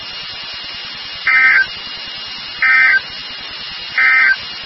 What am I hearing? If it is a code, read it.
Emergency Alert System (SAME) tones, heard as harsh buzzing bursts. It is heard from 1.2 to 4.4 s.
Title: NOAA Weather Radio WXK36